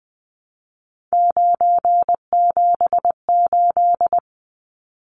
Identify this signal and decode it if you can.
Morse code, heard as a keyed tone, about 20 wpm, 700 Hz. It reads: 978
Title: Numbers - 3-Digit 20wpm